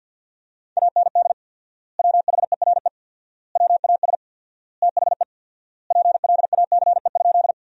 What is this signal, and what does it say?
Morse code, heard as a keyed tone, about 50 wpm, 700 Hz. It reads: AND WHERE WAS THE PLACE?
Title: 4x Word-Spacing - Sentences from Top 200 Words - 50wpm